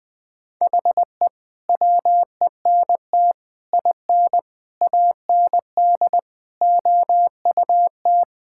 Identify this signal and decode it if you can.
Morse code, heard as a keyed tone, about 21 wpm, 700 Hz. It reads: HE WENT IN AND OUT
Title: Sentences with 5-Word Phrases - 20wpm